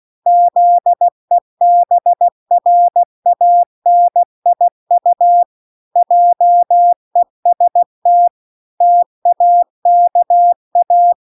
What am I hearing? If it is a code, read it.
Morse code, heard as a keyed tone, about 16 wpm, 700 Hz. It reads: ZEBRANIU JEST TAKA